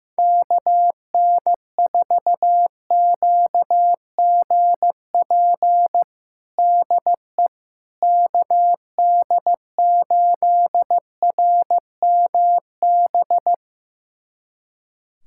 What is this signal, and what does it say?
Morse code, heard as a keyed tone, about 15 wpm, 700 Hz. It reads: KN4QGP DE KD8RMB
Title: USA Callsign DE Callsign 15wpm